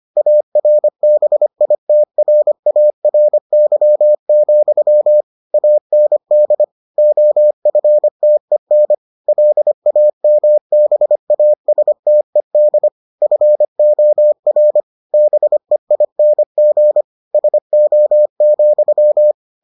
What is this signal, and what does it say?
Morse code, heard as a keyed tone, about 25 wpm, 600 Hz. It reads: ARBITRARY, AND OFTEN LAMBASTED FOR BEING SO,